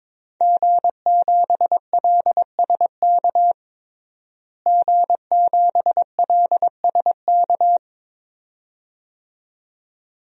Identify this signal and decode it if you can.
Morse code, heard as a keyed tone, about 22 wpm, 700 Hz. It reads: G7LSK G7LSK
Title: DXCC-100 Call Signs - 5 Characters - Repeated 2x - 22wpm